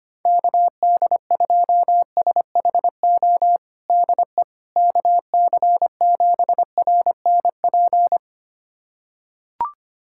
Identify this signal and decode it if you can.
Morse code, heard as a keyed tone, about 25 wpm, 700 Hz. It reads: KD2SHO DE KC7RNP
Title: USA Callsign DE Callsign - 6 Characters - 25wpm